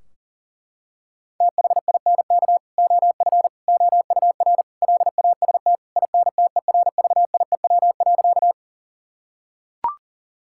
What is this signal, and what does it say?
Morse code, heard as a keyed tone, about 37 wpm, 700 Hz. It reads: THINK OF OUR LAST INTERVIEW.
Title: Sentences from Top 1600 Words 40wpm